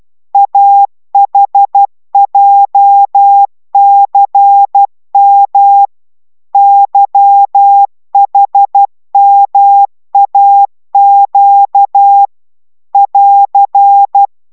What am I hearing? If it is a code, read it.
Morse code, heard as a keyed tone, about 12 wpm, 800 Hz. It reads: AHJCM YHMAQ <AR>